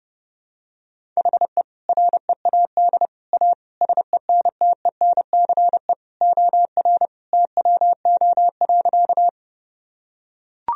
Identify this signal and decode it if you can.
Morse code, heard as a keyed tone, about 30 wpm, 700 Hz. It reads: HE READ A SENTENCE OR TWO.
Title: Sentences from Top 1000 Words - Review Part 1 - 30wpm